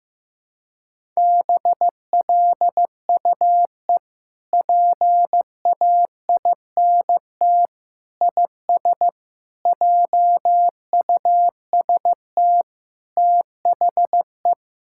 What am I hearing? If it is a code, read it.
Morse code, heard as a keyed tone, about 15 wpm, 700 Hz. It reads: BLUE PAINT IS JUST THE
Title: Sentences from Top 1000 Words - Review Part 6 - 15wpm